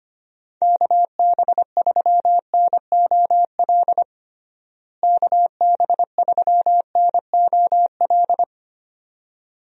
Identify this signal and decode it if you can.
Morse code, heard as a keyed tone, about 25 wpm, 700 Hz. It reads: KB3NOL KB3NOL
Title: USA Call Signs - 6 Characters - Repeated 2x - 25wpm